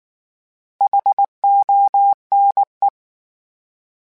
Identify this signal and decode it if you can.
Morse code, heard as a keyed tone, about 19 wpm, 800 Hz. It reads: HONE
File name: H Prefixes / H mp3s